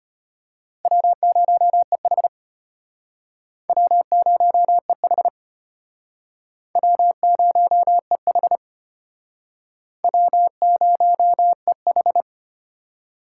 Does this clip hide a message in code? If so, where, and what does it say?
Yes: Morse code from 0.8 to 12.2 s, heard as a keyed tone, about 34 wpm, 700 Hz. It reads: W0EH W0EH W0EH W0EH